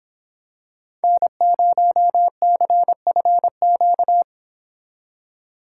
Morse code, heard as a keyed tone, about 26 wpm, 700 Hz. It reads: N0CFQ